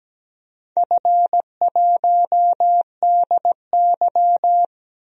Morse code, heard as a keyed tone, about 17 wpm, 700 Hz. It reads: F1DY
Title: DXCC-100 Call Signs - 4 Characters - 17wpm